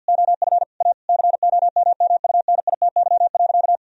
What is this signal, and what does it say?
Morse code, heard as a keyed tone, about 50 wpm, 700 Hz. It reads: OF A COMMUNITY.
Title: Sentences from Top 1000 Words - Review Part 4 - 50wpm